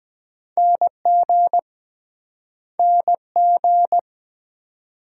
Morse code, heard as a keyed tone, about 23 wpm, 700 Hz. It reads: NG NG